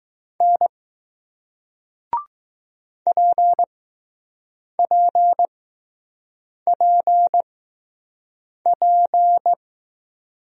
Morse code, heard as a keyed tone, about 20 wpm, spaced as for 6 wpm, 700 Hz. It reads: N PPPP